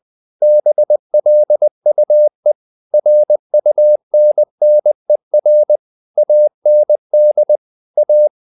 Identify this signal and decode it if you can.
Morse code, heard as a keyed tone, about 20 wpm, 600 Hz. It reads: BLUE RUNNER AND A
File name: the old man and the sea / the old man and the sea part 1 20wpm